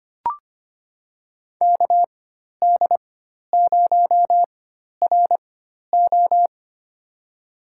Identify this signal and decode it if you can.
Morse code, heard as a keyed tone, about 25 wpm, spaced as for 12 wpm, 700 Hz. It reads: KD0RO
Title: USA Call Signs - 4x Character Spacing - 25wpm